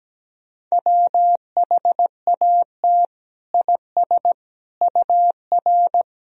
Morse code, heard as a keyed tone, about 17 wpm, 700 Hz. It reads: WHAT IS UR
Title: Sentences from Top 300 Words and Abbreviations 17wpm